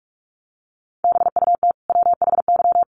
Morse code, heard as a keyed tone, about 44 wpm, 700 Hz. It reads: BUT WHY
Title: Sentences from Top 1400 Words 45wpm